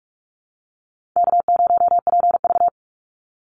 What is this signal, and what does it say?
Morse code, heard as a keyed tone, about 44 wpm, 700 Hz. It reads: K0PV
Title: Call Signs Repeated 3x - 45wpm